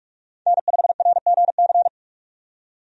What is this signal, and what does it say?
Morse code, heard as a keyed tone, about 44 wpm, 700 Hz. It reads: N5RGC
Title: USA Call Signs - 5 Characters - 45wpm